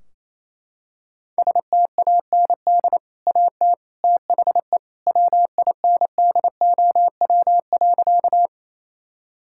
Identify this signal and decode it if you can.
Morse code, heard as a keyed tone, about 28 wpm, 700 Hz. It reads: STAND AT THE WINDOW.
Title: Sentences from Top 1600 Words 28wpm